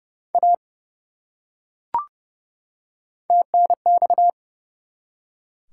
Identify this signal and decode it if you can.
Morse code, heard as a keyed tone, about 30 wpm, 700 Hz. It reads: A TNX
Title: Top 180 QSO Elements 30wpm